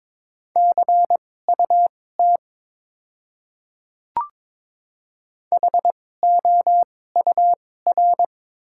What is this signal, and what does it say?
Morse code, heard as a keyed tone, about 22 wpm, spaced as for 16 wpm, 690 Hz. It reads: CUT HOUR
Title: 2x Character Spacing - Top 500 Words - 22wpm